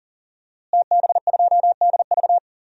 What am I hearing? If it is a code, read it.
Morse code, heard as a keyed tone, about 37 wpm, 700 Hz. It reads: TB2DV
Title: DXCC-100 Call Signs - 5 Characters - Repeated 2x - 40wpm